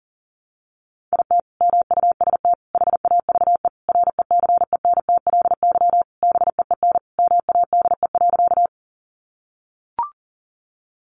Morse code, heard as a keyed tone, about 40 wpm, 700 Hz. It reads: IT MUST HAVE RECENTLY BEEN MADE.
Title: Sentences from Top 1200 Words 40wpm